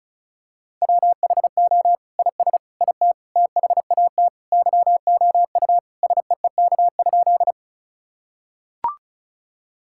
Morse code, heard as a keyed tone, about 36 wpm, 700 Hz. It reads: WHO IS IT THAT YOU SEEK?
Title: Sentences from Top 1600 Words 35wpm